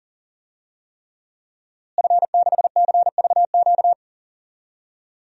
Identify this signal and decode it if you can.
Morse code, heard as a keyed tone, about 40 wpm, 700 Hz. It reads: F6CVQ
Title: DXCC-100 Call Signs - 5 Characters - Repeated 3x - 40wpm